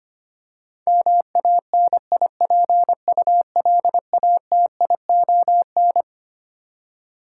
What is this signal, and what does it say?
Morse code, heard as a keyed tone, about 25 wpm, 700 Hz. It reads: MANIPULATION
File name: Twelve Letter Words - 25wpm